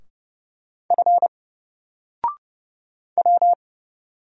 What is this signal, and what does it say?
Morse code, heard as a keyed tone, about 29 wpm, 700 Hz. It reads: F W
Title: Single Character (Letter:Number:Punctuation:Prosign) 30wpm